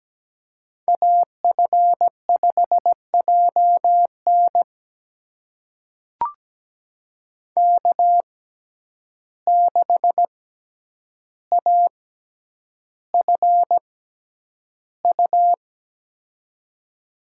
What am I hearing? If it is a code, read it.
Morse code, heard as a keyed tone, about 17 wpm, spaced as for 9 wpm, 700 Hz. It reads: AF5JN K6AFU
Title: USA Call Signs - 5 Characters - 6x Character Spacing - 17wpm